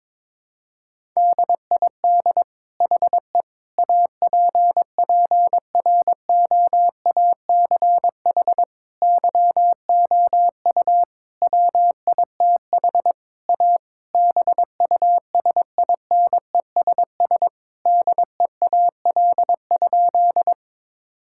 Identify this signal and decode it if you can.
Morse code, heard as a keyed tone, about 22 wpm, 700 Hz. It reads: DID HE APPROACH YOU WITH A BUSINESS DEAL?
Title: Sentences from Top 1000 Words - Review Part 6 - 22wpm